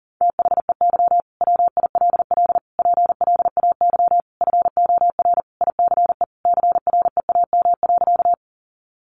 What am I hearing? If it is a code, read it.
Morse code, heard as a keyed tone, about 40 wpm, 700 Hz. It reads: THEY WILL PLAY FOR ICE CREAM.
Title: Sentences from Top 1500 Words 40wpm